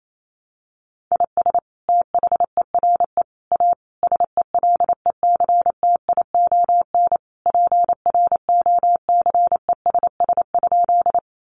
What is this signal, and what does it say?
Morse code, heard as a keyed tone, about 28 wpm, 700 Hz. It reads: IS THERE A SELECTION PROCESS?
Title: Sentences from Top 1400 Words 28wpm